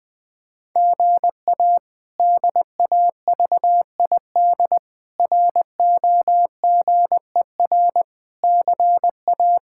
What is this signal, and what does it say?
Morse code, heard as a keyed tone, about 20 wpm, 700 Hz. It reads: GA DAVID ROGER CA
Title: K1USN SST QSO Practice - 20wpm